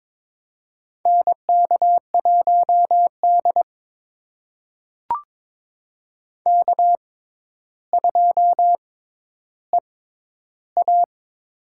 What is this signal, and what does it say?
Morse code, heard as a keyed tone, about 22 wpm, spaced as for 11 wpm, 700 Hz. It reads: NK1D K2EA
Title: USA Call Signs - 4 Characters - 6x Character Spacing - 22wpm